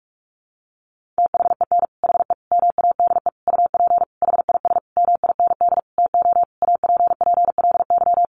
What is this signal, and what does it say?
Morse code, heard as a keyed tone, about 44 wpm, 700 Hz. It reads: THEN HE MADE UP HIS MIND TO APPLY
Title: Sentences from Top 1400 Words 45wpm